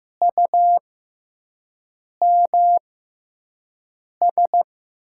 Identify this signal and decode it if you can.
Morse code, heard as a keyed tone, about 15 wpm, spaced as for 5 wpm, 700 Hz. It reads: UMS
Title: USA Call Signs - 5 Characters - 6x Character Spacing - 15wpm